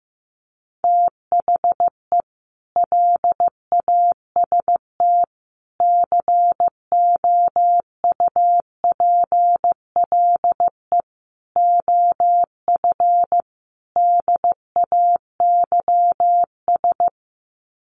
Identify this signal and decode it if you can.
Morse code, heard as a keyed tone, about 15 wpm, 700 Hz. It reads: THE LAST COUPLE OF DAYS
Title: Sets of 5 Words Encore 15wpm